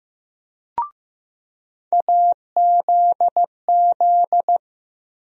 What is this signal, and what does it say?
Morse code, heard as a keyed tone, about 15 wpm, 700 Hz. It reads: AZZ
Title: USA Call Sign Postfixes - 3 Letters 15wpm